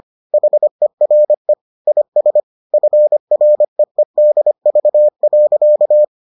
Morse code, heard as a keyed tone, about 25 wpm, 600 Hz. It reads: HERE IS FREEDV.